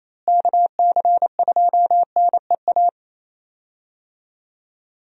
Morse code, heard as a keyed tone, about 28 wpm, 700 Hz. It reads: KC2NEA
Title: USA Call Signs - 6 Characters - Repeated 2x - 28wpm